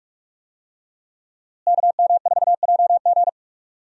Morse code, heard as a keyed tone, about 46 wpm, 700 Hz. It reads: KM4JG
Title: USA Call Signs - 5 Characters - Repeated 2x - 45wpm